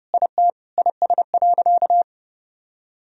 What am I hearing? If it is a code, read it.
Morse code, heard as a keyed tone, about 31 wpm, 700 Hz. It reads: IT IS.
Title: Sentences from Top 1000 Words - Review Part 3 - 30wpm